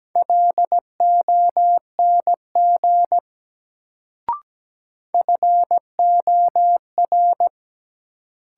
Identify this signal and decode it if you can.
Morse code, heard as a keyed tone, about 17 wpm, 700 Hz. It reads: LONG FOR
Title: Sentences from Top 100 Words - New Word Drill - 17wpm